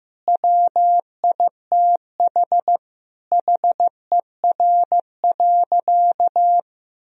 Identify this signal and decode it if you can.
Morse code, heard as a keyed tone, about 15 wpm, 700 Hz. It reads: WITH HER.
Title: Sentences from Top 1000 Words - Review Part 4 - 15wpm